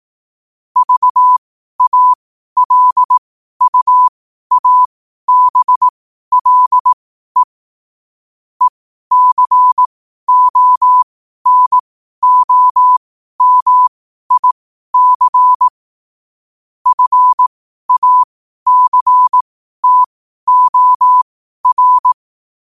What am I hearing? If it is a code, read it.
Morse code, heard as a keyed tone, about 18 wpm, spaced as for 12 wpm, 1000 Hz. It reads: VALUABLE ECONOMIC FACTOR
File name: qotdfiles / 12wpm